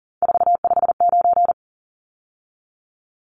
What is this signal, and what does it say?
Morse code, heard as a keyed tone, about 40 wpm, 700 Hz. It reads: 459
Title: Numbers - 3-Digit 40wpm